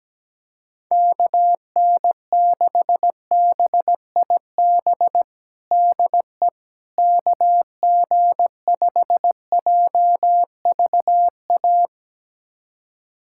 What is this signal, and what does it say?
Morse code, heard as a keyed tone, about 17 wpm, 700 Hz. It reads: KN6BIB DE KG5JVA
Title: USA Callsign DE Callsign - 6 Characters - 17wpm